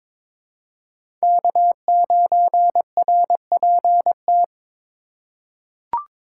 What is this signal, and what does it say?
Morse code, heard as a keyed tone, about 22 wpm, 700 Hz. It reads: K9RPT
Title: USA Call Signs - 3x Character Spacing - 22wpm